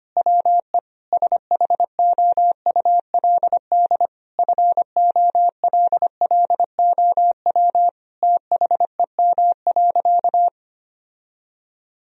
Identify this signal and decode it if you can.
Morse code, heard as a keyed tone, about 25 wpm, 700 Hz. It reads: WE SHOULD FOLLOW THEM.
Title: Sentences from Top 1000 Words - Review Part 1 - 25wpm